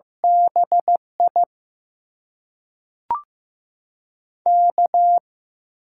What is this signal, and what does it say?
Morse code, heard as a keyed tone, about 15 wpm, 700 Hz. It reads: BI K
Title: USA Call Signs - 6 Characters - 8x Character Spacing - 15wpm